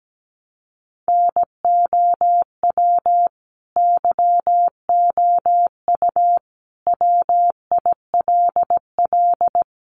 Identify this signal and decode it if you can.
Morse code, heard as a keyed tone, about 17 wpm, 700 Hz. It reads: NOW YOU WILL
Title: Sentences from Top 1200 Words 17wpm